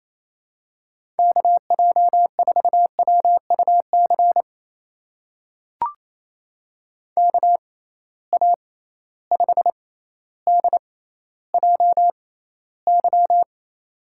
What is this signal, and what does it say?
Morse code, heard as a keyed tone, about 28 wpm, spaced as for 14 wpm, 700 Hz. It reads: KJ4WUC KA5DJY
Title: USA Call Signs - 6 Characters - 6x Character Spacing - 28wpm